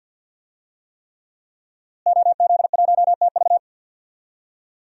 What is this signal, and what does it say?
Morse code, heard as a keyed tone, about 51 wpm, 700 Hz. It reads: OZ1TV
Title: DXCC-100 Call Signs - 5 Characters - Repeated 3x - 50wpm